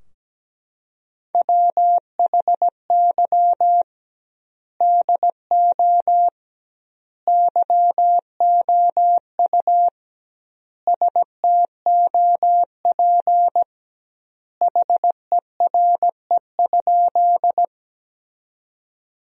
Morse code, heard as a keyed tone, about 17 wpm, 700 Hz. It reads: WHY DO YOU STOP HERE?
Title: 2x Word-Spacing - Sentences from Top 300 Words - 17wpm